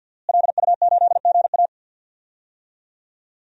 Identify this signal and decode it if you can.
Morse code, heard as a keyed tone, about 53 wpm, 700 Hz. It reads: LU8GA
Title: DXCC-100 Call Signs - 5 Characters - Repeated 2x - 50wpm